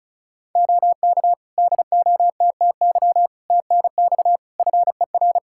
Morse code, heard as a keyed tone, about 36 wpm, 700 Hz. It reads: OK DOTTY TNX FER
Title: K1USN SST QSO Practice 35wpm